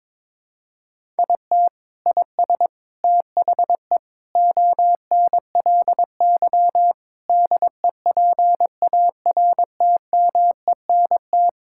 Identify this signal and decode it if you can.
Morse code, heard as a keyed tone, about 22 wpm, 700 Hz. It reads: IT IS THE ONLY DEPARTMENT
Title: Sentences from Top 1000 Words - Review Part 5 - 22wpm